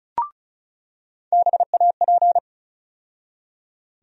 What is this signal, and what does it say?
Morse code, heard as a keyed tone, about 36 wpm, 700 Hz. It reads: BAP
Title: USA Call Sign Postfixes - 3 Letters 35wpm